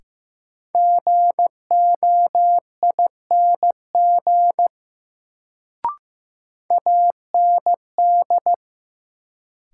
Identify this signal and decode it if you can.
Morse code, heard as a keyed tone, about 15 wpm, 700 Hz. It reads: GOING AND
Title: Sets of 2 Words - New Word Drill - 15wpm